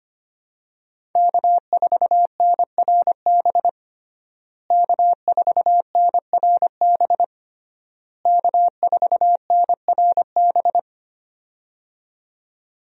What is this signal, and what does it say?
Morse code, heard as a keyed tone, about 25 wpm, 700 Hz. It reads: K4NRB K4NRB K4NRB
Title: USA Call Signs - 5 Characters - Repeated 3x - 25wpm